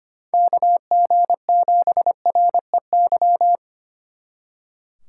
Morse code, heard as a keyed tone, about 31 wpm, 700 Hz. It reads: KG7REY
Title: Speed Racing - USA 2x3 Call Signs 25wpm